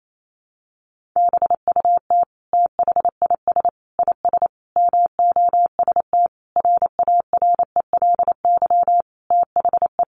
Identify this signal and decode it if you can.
Morse code, heard as a keyed tone, about 28 wpm, 700 Hz. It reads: BUT THIS IS MOST RARELY THE
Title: Sentences from Top 1400 Words 28wpm